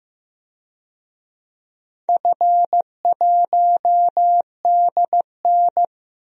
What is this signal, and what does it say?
Morse code, heard as a keyed tone, about 15 wpm, 700 Hz. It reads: F1DN
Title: DXCC-100 Call Signs - 4 Characters - Repeated 2x - 15wpm